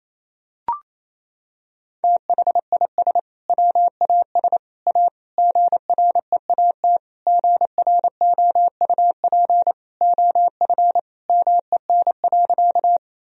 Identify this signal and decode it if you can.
Morse code, heard as a keyed tone, about 28 wpm, 700 Hz. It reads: THIS WAS A GREAT GROUP OF MEN.
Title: Sentences from Top 1000 Words - Review Part 1 - 28wpm